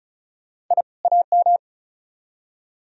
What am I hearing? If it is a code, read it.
Morse code, heard as a keyed tone, about 46 wpm, 700 Hz. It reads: I AM